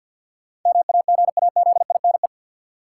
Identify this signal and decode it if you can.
Morse code, heard as a keyed tone, about 53 wpm, 700 Hz. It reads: MAGAZINE
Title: Sentences from Top 1000 Words - Review Part 7 - New Word Drill - 50wpm